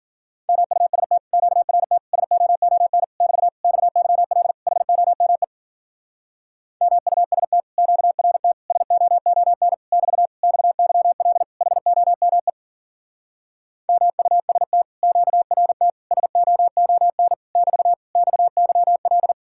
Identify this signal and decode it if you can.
Morse code, heard as a keyed tone, about 48 wpm, 700 Hz. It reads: MUST QRT SOON <BT> XYL HOME MUST QRT SOON <BT> XYL HOME MUST QRT SOON <BT> XYL
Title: Speed-Racing Common QSO Phrases - 40wpm